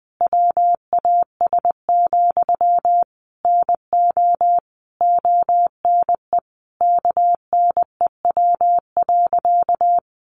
Morse code, heard as a keyed tone, about 20 wpm, 700 Hz. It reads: WAS, NO ONE KNEW.
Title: Sentences from Top 400 Words 20wpm